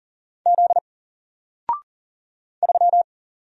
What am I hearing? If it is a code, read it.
Morse code, heard as a keyed tone, about 40 wpm, 700 Hz. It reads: Z 3
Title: Single Letter-Number Alt 40wpm